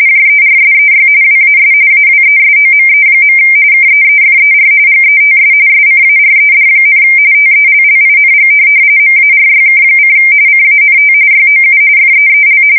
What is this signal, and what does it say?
RTTY, about 45.45 baud. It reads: JERSEY WAS AWESOME. KEEFER GETS SCARED, TELLING MARYK: CANT YOU FEEL
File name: fsk170 / USS-New-Jersey